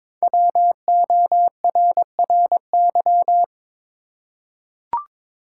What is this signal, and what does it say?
Morse code, heard as a keyed tone, about 23 wpm, 700 Hz. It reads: WORRY
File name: 5 Letter Words - First 3 Letters Given - 22wpm